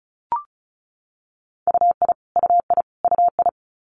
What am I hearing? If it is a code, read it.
Morse code, heard as a keyed tone, about 36 wpm, 700 Hz. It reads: UI UI UI
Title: Multi-letter Phonemes Repeated 3x 35wpm